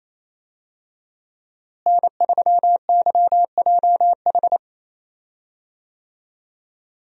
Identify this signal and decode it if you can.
Morse code, heard as a keyed tone, about 28 wpm, 700 Hz. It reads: N3YJH
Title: USA Call Signs - 5 Characters - Repeated 3x - 28wpm